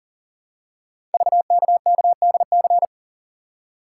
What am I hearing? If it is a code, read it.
Morse code, heard as a keyed tone, about 38 wpm, 700 Hz. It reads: VKKDC
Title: Code Groups - 5 Letters - 40wpm